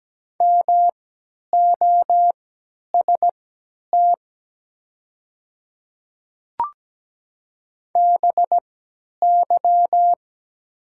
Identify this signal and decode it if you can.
Morse code, heard as a keyed tone, about 17 wpm, spaced as for 10 wpm, 700 Hz. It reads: MOST BY
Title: 3x Character Spacing - Top 200 Words - 17wpm